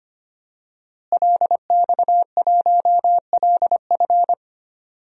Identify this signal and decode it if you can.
Morse code, heard as a keyed tone, about 25 wpm, 700 Hz. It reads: LX1LF
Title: DXCC-100 Call Signs - 5 Characters - Repeated 3x - 25wpm